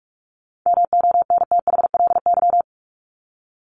Morse code, heard as a keyed tone, about 44 wpm, 700 Hz. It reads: MONTHLY
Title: LY Suffix - Top 100 Words with Base Words 45wpm